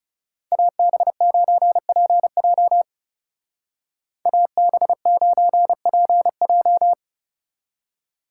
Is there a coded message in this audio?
Yes: Morse code from 0.5 to 6.9 s, heard as a keyed tone, about 39 wpm, 700 Hz. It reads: AB9PJ AB9PJ